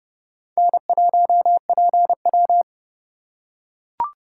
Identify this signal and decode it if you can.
Morse code, heard as a keyed tone, about 31 wpm, 700 Hz. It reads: N1PW